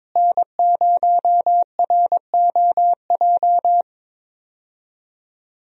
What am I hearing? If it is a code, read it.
Morse code, heard as a keyed tone, about 22 wpm, 700 Hz. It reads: N0ROJ
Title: USA Call Signs - 5 Characters - Repeated 3x - 22wpm